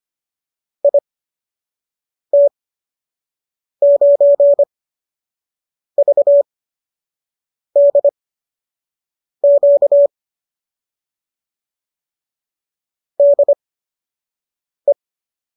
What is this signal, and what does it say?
Morse code, heard as a keyed tone, about 25 wpm, spaced as for 6 wpm, 570 Hz. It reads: IT9VDQ DE